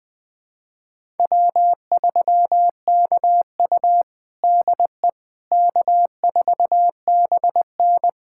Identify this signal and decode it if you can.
Morse code, heard as a keyed tone, about 20 wpm, 700 Hz. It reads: W3KU DE K4BN
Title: USA Callsign DE Callsign - 4 Characters - 20wpm